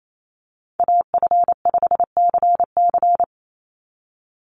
Morse code, heard as a keyed tone, about 28 wpm, 700 Hz. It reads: AF5CC
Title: Call Signs Repeated 3x - 28wpm